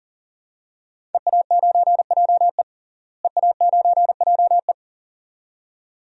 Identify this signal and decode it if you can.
Morse code, heard as a keyed tone, about 39 wpm, 700 Hz. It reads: EA9JE EA9JE
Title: DXCC-100 Call Signs - 5 Characters - Repeated 2x - 40wpm